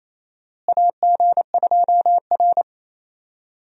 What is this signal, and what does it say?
Morse code, heard as a keyed tone, about 28 wpm, 700 Hz. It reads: AG2R
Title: USA Call Signs - 4 Characters - Repeated 3x - 28wpm